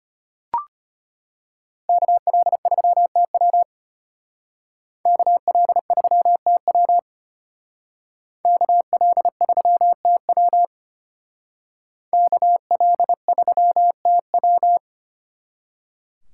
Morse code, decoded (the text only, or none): KL3TW KL3TW KL3TW KL3TW